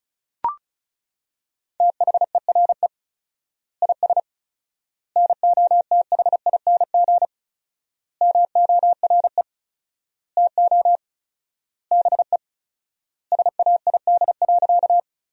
Morse code, heard as a keyed tone, about 36 wpm, 700 Hz. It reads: THERE IS NOTHING MORE TO BE SAID.
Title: 4x Word-Spacing - Sentences from Top 500 Words - 35wpm